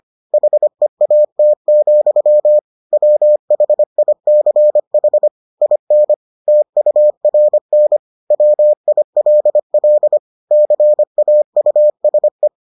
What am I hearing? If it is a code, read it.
Morse code, heard as a keyed tone, about 25 wpm, 600 Hz. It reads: HEAT, WHICH IN TURN WILL CAUSE